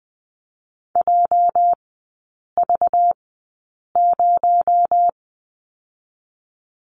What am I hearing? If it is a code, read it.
Morse code, heard as a keyed tone, about 20 wpm, spaced as for 8 wpm, 700 Hz. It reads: JV0
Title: 218 Lesson 35e - Number 0 - Three Characters 20wpm